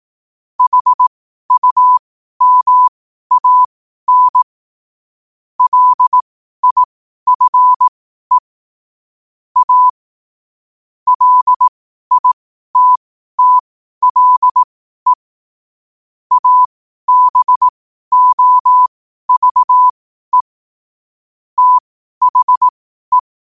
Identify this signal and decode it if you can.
Morse code, heard as a keyed tone, about 18 wpm, spaced as for 12 wpm, 1000 Hz. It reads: HUMAN LIFE A LITTLE ABOVE THE
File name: qotdfiles / 12wpm